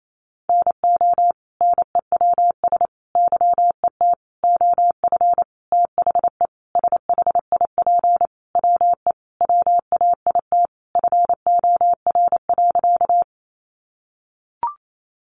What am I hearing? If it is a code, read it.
Morse code, heard as a keyed tone, about 28 wpm, 700 Hz. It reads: NO NEWS YET OF THE SHIP WE WAIT FOR.
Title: Sentences from Top 800 Words 28wpm